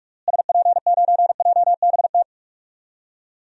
Morse code, heard as a keyed tone, about 42 wpm, 700 Hz. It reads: SP9JBT
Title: DXCC-100 Call Signs - 6 Characters - Repeated 3x - 45wpm